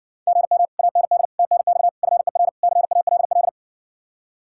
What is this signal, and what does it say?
Morse code, heard as a keyed tone, about 60 wpm, 700 Hz. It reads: GM AND TNX FER CALL